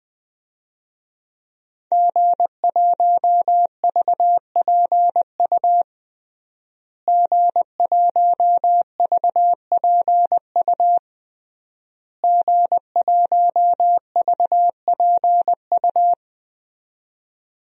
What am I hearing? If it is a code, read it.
Morse code, heard as a keyed tone, about 21 wpm, 700 Hz. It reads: G1VPU G1VPU G1VPU
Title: DXCC-100 Call Signs - 5 Characters - Repeated 3x - 20wpm